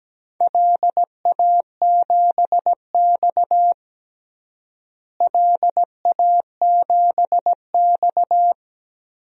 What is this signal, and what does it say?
Morse code, heard as a keyed tone, about 17 wpm, 700 Hz. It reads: LA7X LA7X
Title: DXCC-100 Call Signs - 4 Characters - Repeated 3x - 17wpm